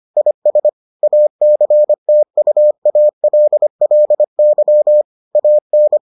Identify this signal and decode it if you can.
Morse code, heard as a keyed tone, about 25 wpm, 600 Hz. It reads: IS ACTUALLY AN